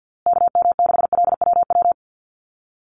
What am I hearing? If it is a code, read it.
Morse code, heard as a keyed tone, about 50 wpm, 700 Hz. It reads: KM6LWW
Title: USA Call Signs Repeated 2x 50wpm